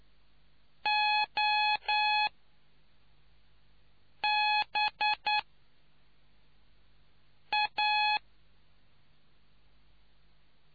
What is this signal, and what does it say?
Morse code, heard as a keyed tone, about 9 wpm, spaced as for 4 wpm, 840 Hz. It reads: OBA